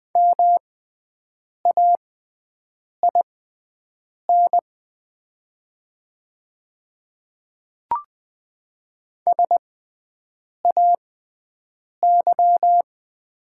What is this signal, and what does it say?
Morse code, heard as a keyed tone, about 20 wpm, spaced as for 7 wpm, 700 Hz. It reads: MAIN SAY
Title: 6x Character Spacing - Top 500 Words - 20wpm